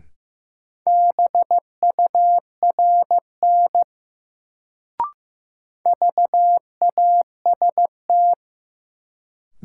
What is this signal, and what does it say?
Morse code, heard as a keyed tone, about 15 wpm, 700 Hz. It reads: BURN VAST
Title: Sentences from Top 1300 Words - New Word Drill Alt - 15wpm